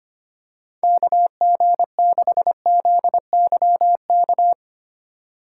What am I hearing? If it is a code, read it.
Morse code, heard as a keyed tone, about 25 wpm, 700 Hz. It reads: KG6ZYK
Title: USA Call Signs - 4x Character Spacing - 25wpm